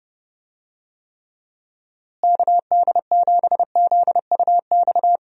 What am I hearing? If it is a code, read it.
Morse code, heard as a keyed tone, about 30 wpm, 700 Hz. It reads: KD7ZUX
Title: USA Call Signs - 6 Characters - Repeated 2x - 30wpm